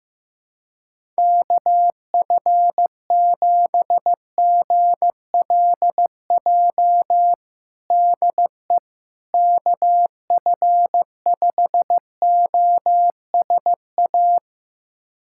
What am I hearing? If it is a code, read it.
Morse code, heard as a keyed tone, about 15 wpm, 700 Hz. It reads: KF7GLJ DE KF5OSA
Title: USA Callsign DE Callsign - 6 Characters - 15wpm